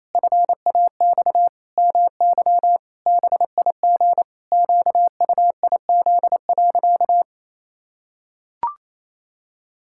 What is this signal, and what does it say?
Morse code, heard as a keyed tone, about 28 wpm, 700 Hz. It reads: FAX MY BIG QUIZ.
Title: Pangram Challenge v2 - 28wpm